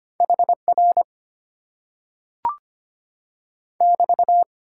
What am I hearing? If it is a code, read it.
Morse code, heard as a keyed tone, about 25 wpm, 700 Hz. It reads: HR <BT>
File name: Top 90 QSO Elements - 8x Character Spacing - 25wpm